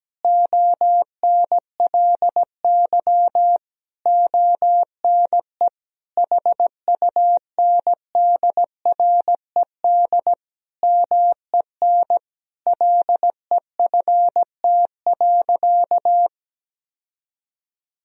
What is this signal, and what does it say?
Morse code, heard as a keyed tone, about 17 wpm, 700 Hz. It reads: ONLY ONE HUNDRED MEN LEFT.
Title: Sentences from Top 1000 Words - Review Part 2 - 17wpm